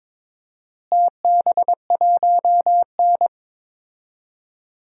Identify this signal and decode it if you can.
Morse code, heard as a keyed tone, about 22 wpm, 700 Hz. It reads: TB1N
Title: DXCC-100 Call Signs - 4 Characters - Repeated 3x - 22wpm